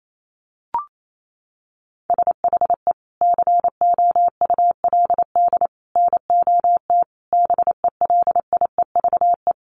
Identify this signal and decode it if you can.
Morse code, heard as a keyed tone, about 28 wpm, 700 Hz. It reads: SHE COULD NOT BELIEVE
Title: Sentences from Top 1000 Words 28wpm